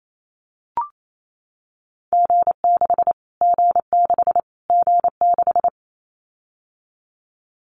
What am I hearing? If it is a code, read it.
Morse code, heard as a keyed tone, about 28 wpm, 700 Hz. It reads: G6 G6 G6
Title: Callsign Prefixes for Top 25 DXCC - Repeated 3x - 28wpm